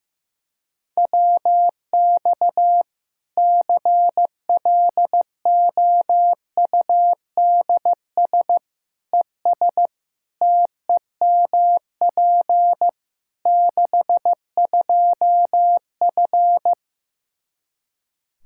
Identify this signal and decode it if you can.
Morse code, heard as a keyed tone, about 16 wpm, 700 Hz. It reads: WX CLOUDS ES TEMP 62F
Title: K1USN SST QSO Practice 15wpm